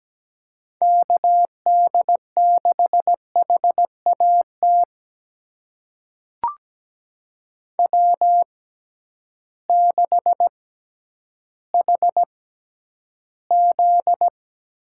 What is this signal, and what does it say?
Morse code, heard as a keyed tone, about 17 wpm, spaced as for 10 wpm, 700 Hz. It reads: KD6HAT W6HZ
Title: USA Call Signs - 6x Character Spacing - 17wpm